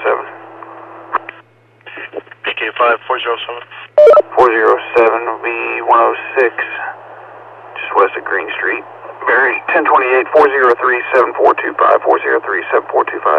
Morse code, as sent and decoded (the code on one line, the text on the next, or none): none